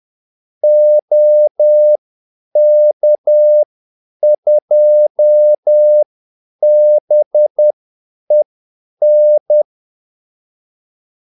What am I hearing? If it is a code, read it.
Morse code, heard as a keyed tone, about 10 wpm, spaced as for 8 wpm, 600 Hz. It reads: OK2BEN
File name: Callsigns-10-8wpm